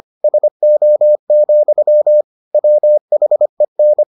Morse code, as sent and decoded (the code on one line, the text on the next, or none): ... --- --..-- / .-- .... -.
SO, WHEN